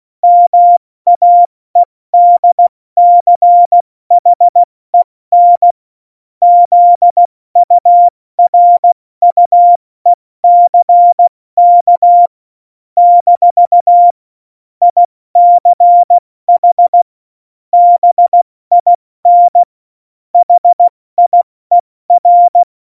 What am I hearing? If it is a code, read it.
Morse code, heard as a keyed tone, about 16 wpm, spaced as for 14 wpm, 700 Hz. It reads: MAEDCHEN ZURUECK - ICH BIN HIER
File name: Morsekurs / Witze / Witz 31. Oktober 17, Tempo 70